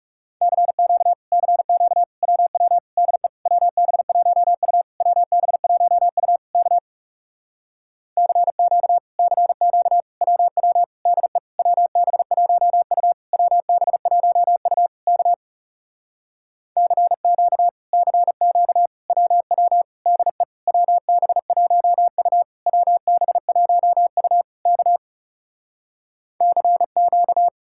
Morse code, heard as a keyed tone, about 38 wpm, 700 Hz. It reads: CQ CQ WW DE WB1U WB1U K CQ CQ WW DE WB1U WB1U K CQ CQ WW DE WB1U WB1U K CQ
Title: Speed-Racing Common QSO Phrases - 30wpm